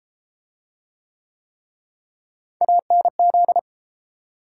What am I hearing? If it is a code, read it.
Morse code, heard as a keyed tone, about 31 wpm, 700 Hz. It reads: ANZ